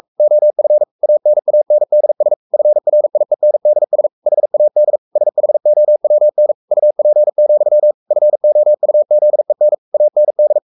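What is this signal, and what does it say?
Morse code, heard as a keyed tone, about 44 wpm, 600 Hz. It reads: OF ANANDS FRIENDS HAD SHOWN UP, FOUZEN AND
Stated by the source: Morse code at 43 wpm sending Ken Follett's novel Never in English, from CW med Gnister.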